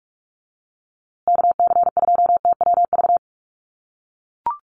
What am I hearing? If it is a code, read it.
Morse code, heard as a keyed tone, about 44 wpm, 700 Hz. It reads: KC2TWV
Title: USA Call Signs Repeated 3x 45wpm